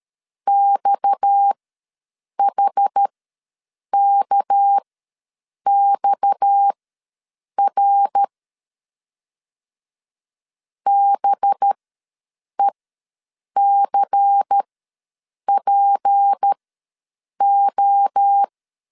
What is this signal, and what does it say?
Morse code, heard as a keyed tone, about 13 wpm, spaced as for 7 wpm, 800 Hz. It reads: XHKXR BECPO
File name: LEZIONI / 6 PKX0 / 6 PKX ws 12wpm 3106